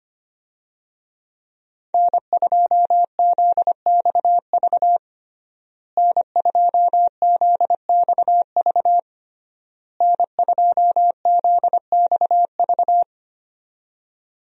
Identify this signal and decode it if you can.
Morse code, heard as a keyed tone, about 25 wpm, 700 Hz. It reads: N2ZXV N2ZXV N2ZXV
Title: USA Call Signs - 5 Characters - Repeated 3x - 25wpm